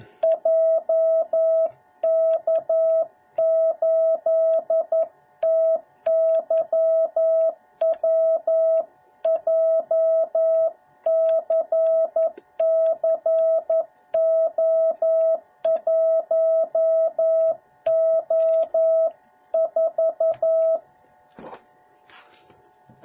Morse code, heard as a keyed tone, about 11 wpm, 650 Hz. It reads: JK8TYW JCCO1O4